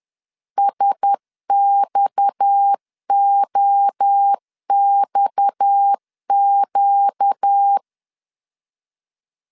Morse code, heard as a keyed tone, about 10 wpm, 800 Hz. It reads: SXOXQ
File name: LEZIONI / 6 PKX0 / 6 PKX 10wpm 2426